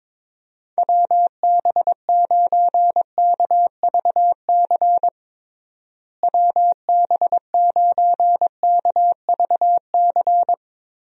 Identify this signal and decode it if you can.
Morse code, heard as a keyed tone, about 22 wpm, 700 Hz. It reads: WB9KVC WB9KVC
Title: USA Call Signs - 6 Characters - Repeated 3x - 22wpm